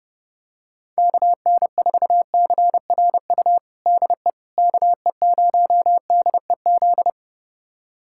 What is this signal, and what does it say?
Morse code, heard as a keyed tone, about 30 wpm, 700 Hz. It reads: KN4CRU DE KE0DEZ
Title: USA Callsign DE Callsign - 6 Characters - 30wpm